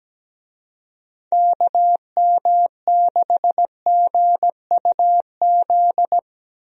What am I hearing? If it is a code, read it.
Morse code, heard as a keyed tone, about 17 wpm, 700 Hz. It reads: KM6GUZ
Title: USA Call Signs - 6 Characters - Repeated 2x - 17wpm